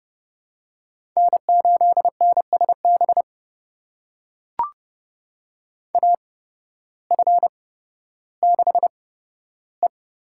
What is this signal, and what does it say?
Morse code, heard as a keyed tone, about 32 wpm, spaced as for 14 wpm, 700 Hz. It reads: N8NSB AF6E
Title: USA Call Signs - 5 Characters - 8x Character Spacing - 30wpm